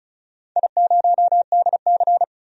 Morse code, heard as a keyed tone, about 36 wpm, 700 Hz. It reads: I0DC
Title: DXCC-100 Call Signs - 4 Characters - 35wpm